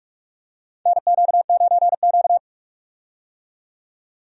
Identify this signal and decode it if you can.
Morse code, heard as a keyed tone, about 42 wpm, 700 Hz. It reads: NQ9Q
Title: USA Call Signs - 4 Characters - 45wpm